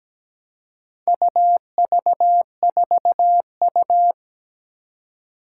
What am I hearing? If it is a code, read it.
Morse code, heard as a keyed tone, about 17 wpm, 700 Hz. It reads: UV4U
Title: Four Challenging Letter-Number Combinations 17wpm